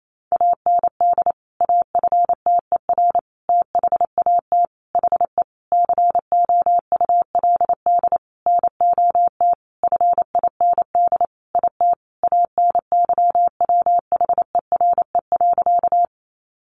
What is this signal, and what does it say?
Morse code, heard as a keyed tone, about 28 wpm, 700 Hz. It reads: AND AFTER THAT HE COULD NOT FIND IT ANYWHERE.
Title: Sentences from Top 1400 Words 28wpm